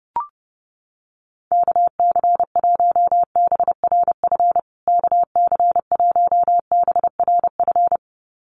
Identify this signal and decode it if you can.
Morse code, heard as a keyed tone, about 28 wpm, 700 Hz. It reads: KC1BRF KC1BRF
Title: USA Call Signs Repeated 2x 30wpm